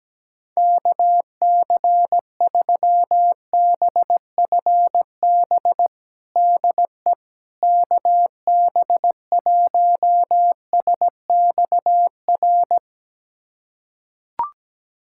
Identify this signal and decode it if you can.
Morse code, heard as a keyed tone, about 17 wpm, 700 Hz. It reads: KC3BFB DE KB1SXR
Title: USA Callsign DE Callsign - 6 Characters - 17wpm